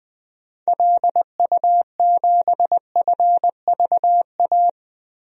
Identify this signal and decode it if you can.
Morse code, heard as a keyed tone, about 20 wpm, 700 Hz. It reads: LU7FVA
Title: DXCC-100 Call Signs - 6 Characters - 20wpm